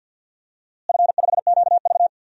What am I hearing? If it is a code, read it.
Morse code, heard as a keyed tone, about 52 wpm, 700 Hz. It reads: F5QV
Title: DXCC-100 Call Signs - 4 Characters - Repeated 3x - 50wpm